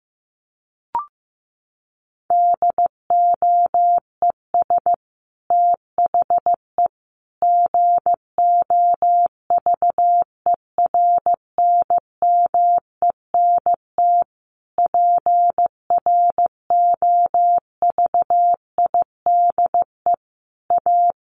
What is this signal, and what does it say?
Morse code, heard as a keyed tone, about 15 wpm, 700 Hz. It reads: DOES THE GOVERNMENT PROVIDE A
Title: Sentences from Top 1500 Words 15wpm